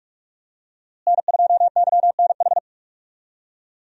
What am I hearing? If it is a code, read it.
Morse code, heard as a keyed tone, about 46 wpm, 700 Hz. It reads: N2YNH
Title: USA Call Signs - 5 Characters - Repeated 3x - 45wpm